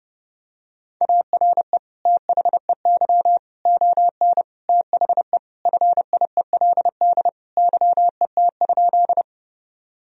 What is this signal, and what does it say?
Morse code, heard as a keyed tone, about 30 wpm, 700 Hz. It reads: ARE THEY ON THE FIELD YET?
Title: Sentences from Top 1000 Words - Review Part 2 - 30wpm